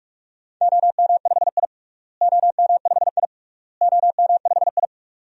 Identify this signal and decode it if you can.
Morse code, heard as a keyed tone, about 45 wpm, 700 Hz. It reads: OM5I OM5I OM5I
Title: DXCC-100 Call Signs - 4 Characters - Repeated 3x - 45wpm